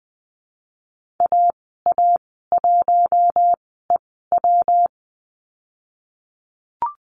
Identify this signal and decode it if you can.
Morse code, heard as a keyed tone, about 20 wpm, spaced as for 14 wpm, 700 Hz. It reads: AA1EW
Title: 122 Lesson 20g - Number 3 - Call Signs 20wpm